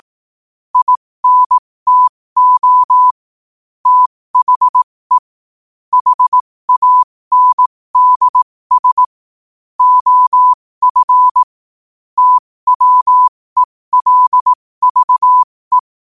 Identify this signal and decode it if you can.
Morse code, heard as a keyed tone, about 18 wpm, spaced as for 15 wpm, 1000 Hz. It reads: INTO THE HANDS OF TWELVE